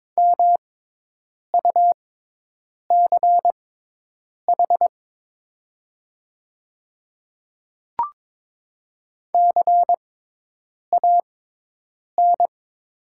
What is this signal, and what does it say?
Morse code, heard as a keyed tone, about 22 wpm, spaced as for 8 wpm, 700 Hz. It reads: MUCH CAN
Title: 6x Character Spacing - Top 300 Words - 22wpm